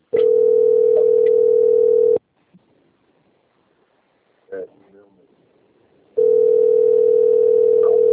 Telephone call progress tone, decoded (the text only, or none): ringback tone